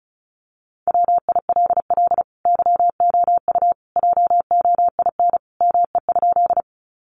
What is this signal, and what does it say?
Morse code, heard as a keyed tone, about 34 wpm, 700 Hz. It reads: WILL YOU JOIN ME?
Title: Sentences from Top 1400 Words 35wpm